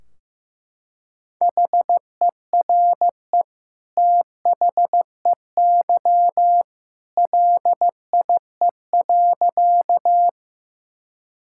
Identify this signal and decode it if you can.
Morse code, heard as a keyed tone, about 14 wpm, 700 Hz. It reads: HERE THEY LIE.
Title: Sentences from Top 1600 Words 15wpm